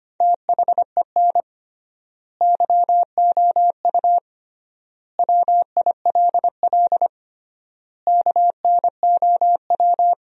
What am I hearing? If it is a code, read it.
Morse code, heard as a keyed tone, about 25 wpm, 700 Hz. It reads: THEN YOU WILL KNOW
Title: 3x Word-Spacing - Sentences from Top 400 Words - 25wpm